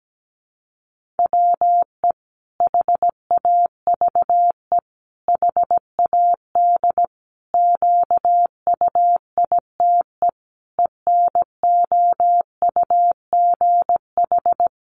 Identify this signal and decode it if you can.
Morse code, heard as a keyed tone, about 17 wpm, 700 Hz. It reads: WE HAVE HAD QUITE ENOUGH
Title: Sentences from Top 700 Words 17wpm